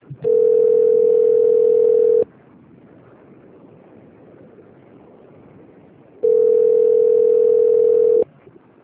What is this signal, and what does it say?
Ringback tone (North America), 440 + 480 Hz.